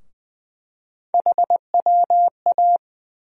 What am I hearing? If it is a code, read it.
Morse code, heard as a keyed tone, about 20 wpm, 700 Hz. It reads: HWA
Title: USA Call Sign Postfixes - 3 Letters 20wpm